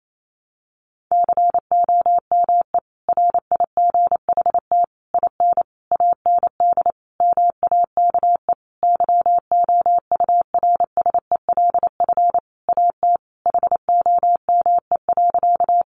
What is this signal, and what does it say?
Morse code, heard as a keyed tone, about 28 wpm, 700 Hz. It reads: COME RIGHT IN AND MAKE YOURSELF AT HOME.
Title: Sentences from Top 600 Words 28wpm